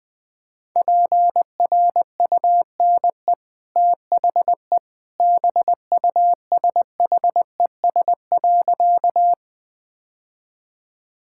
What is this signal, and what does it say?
Morse code, heard as a keyed tone, about 20 wpm, 700 Hz. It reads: PRUNE THE BUSHES.
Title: Easy Sentences with Hints - 20wpm